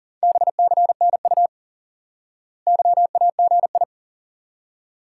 Morse code, heard as a keyed tone, about 40 wpm, 700 Hz. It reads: BCNU YAGI
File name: IQE Course 3 - Lesson 89 - YAGI - Two Elements - 40wpm